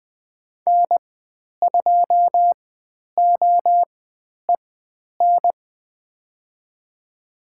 Morse code, heard as a keyed tone, about 20 wpm, spaced as for 10 wpm, 700 Hz. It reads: N2OEN
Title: 095 Lesson 16g - Letter c - Call Signs 20wpm